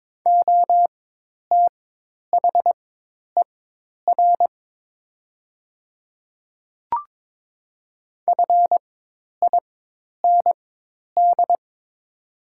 Morse code, heard as a keyed tone, about 22 wpm, spaced as for 10 wpm, 700 Hz. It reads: OTHER FIND
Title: 4x Character Spacing - Top 100 Words - 22wpm